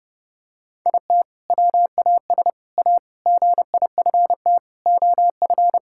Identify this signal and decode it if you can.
Morse code, heard as a keyed tone, about 28 wpm, 700 Hz. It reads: IT WAS A GIFT OF
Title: Sentences from Top 1000 Words - Review Part 6 - 30wpm